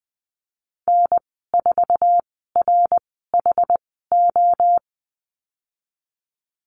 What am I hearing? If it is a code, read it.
Morse code, heard as a keyed tone, about 20 wpm, spaced as for 15 wpm, 700 Hz. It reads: N4RHO
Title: 062 Lesson 11g - Letter h - Call Signs 20wpm